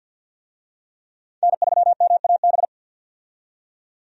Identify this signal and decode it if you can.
Morse code, heard as a keyed tone, about 49 wpm, 700 Hz. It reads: N3MAB
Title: USA Call Signs - 5 Characters - 50wpm